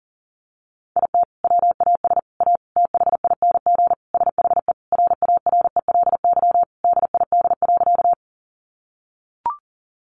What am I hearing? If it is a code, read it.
Morse code, heard as a keyed tone, about 40 wpm, 700 Hz. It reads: IT WAS A THING SHE RARELY DID.
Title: Sentences from Top 1400 Words 40wpm